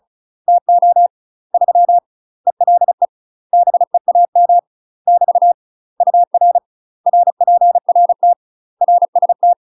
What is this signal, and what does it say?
Morse code, heard as a keyed tone, about 36 wpm, 700 Hz. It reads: TO 3 ELE BEAM <BT> UR RPRT RST
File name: UK MASTER QSOs TRAINING / 35WPM / UK MASTER QSOs 0017